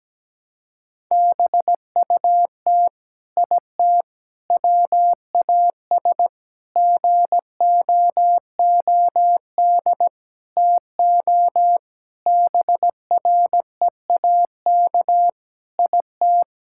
Morse code, heard as a keyed tone, about 17 wpm, 700 Hz. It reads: BUT IT WAS GOOD TO BREAK IT
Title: Sentences from Top 1000 Words - Review Part 6 - 17wpm